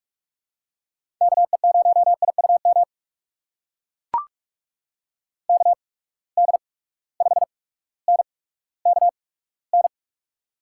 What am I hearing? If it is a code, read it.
Morse code, heard as a keyed tone, about 44 wpm, spaced as for 19 wpm, 700 Hz. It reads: KE0IUM KD5NKN
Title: USA Call Signs - 6 Characters - 8x Character Spacing - 45wpm